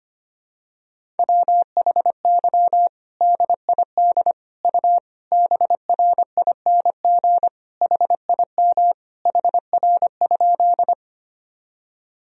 Morse code, heard as a keyed tone, about 25 wpm, 700 Hz. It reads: WHY DID U BRING HIM HR?
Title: Sentences from Top 500 Words and Abbreviations 25wpm